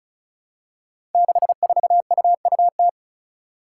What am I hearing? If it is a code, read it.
Morse code, heard as a keyed tone, about 36 wpm, 700 Hz. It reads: 64UUT
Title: Five Challenging Letter-Number Combinations 35wpm